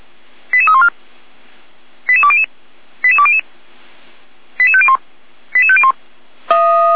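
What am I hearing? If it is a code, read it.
Five-tone selective call (ZVEI), heard as quick runs of five tones, tones of about 70 ms. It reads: 88314 88200 88200 88501 88501
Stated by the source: ZVEI five-tone selective call 88156, received on SDR.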